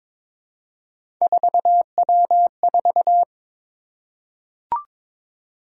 Morse code, heard as a keyed tone, about 23 wpm, 700 Hz. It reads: 4W4
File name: practice / 22wpm / Three Letter-Number Combinations - 22wpm